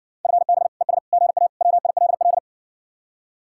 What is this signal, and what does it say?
Morse code, heard as a keyed tone, about 68 wpm, 700 Hz. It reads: FB ES GA WILL